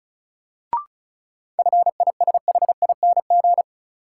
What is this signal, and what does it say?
Morse code, heard as a keyed tone, about 34 wpm, 700 Hz. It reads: FISHING
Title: Sentences from Top 1000 Words - Review Part 5 - New Word Drill - 35wpm